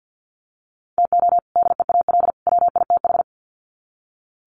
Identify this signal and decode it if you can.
Morse code, heard as a keyed tone, about 48 wpm, 700 Hz. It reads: TO DEAL WITH
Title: Sets of 3 Words 50wpm